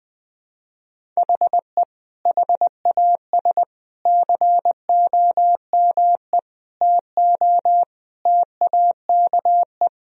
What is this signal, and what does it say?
Morse code, heard as a keyed tone, about 21 wpm, 700 Hz. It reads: HE HAS COME TO TAKE
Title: Sentences from Top 1000 Words - Review Part 1 - 20wpm